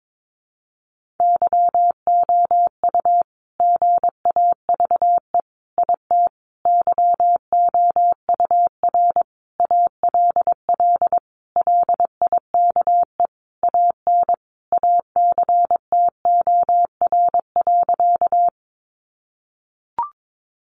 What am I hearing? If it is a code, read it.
Morse code, heard as a keyed tone, about 22 wpm, 700 Hz. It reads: YOU GAVE IT YOUR ALL LIKE AN ACTOR.
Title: Sentences from Top 1500 Words 22wpm